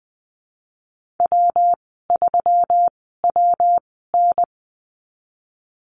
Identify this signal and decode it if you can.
Morse code, heard as a keyed tone, about 20 wpm, spaced as for 14 wpm, 700 Hz. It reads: W3WN
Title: 173 Lesson 28g - Number 9 - Call Signs 20wpm